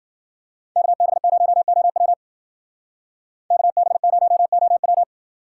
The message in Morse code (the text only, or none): KB0OW KB0OW